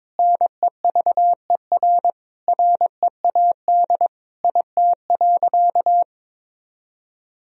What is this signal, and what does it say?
Morse code, heard as a keyed tone, about 22 wpm, 700 Hz. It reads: NEVER READ IT.
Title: Sentences from Top 1000 Words - Review Part 4 - 22wpm